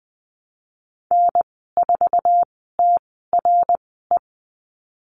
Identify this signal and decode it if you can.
Morse code, heard as a keyed tone, about 20 wpm, spaced as for 15 wpm, 700 Hz. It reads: N4TRE
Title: 055 Lesson 10g - Letter r - Call Signs 20wpm